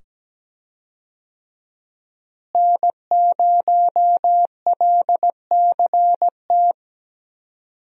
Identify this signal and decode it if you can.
Morse code, heard as a keyed tone, about 17 wpm, 700 Hz. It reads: N0LCT
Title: USA Call Signs - 5 Characters - Repeated 2x - 17wpm